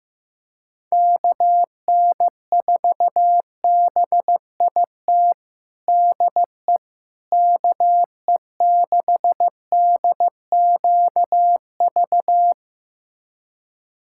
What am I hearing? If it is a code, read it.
Morse code, heard as a keyed tone, about 15 wpm, 700 Hz. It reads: KN4BIT DE KE6DQV
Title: USA Callsign DE Callsign - 6 Characters - 15wpm